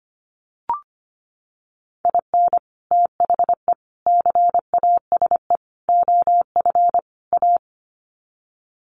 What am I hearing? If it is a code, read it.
Morse code, heard as a keyed tone, about 25 wpm, 700 Hz. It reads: IN THE CASE OF A
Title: Sets of 5 Words Encore 25wpm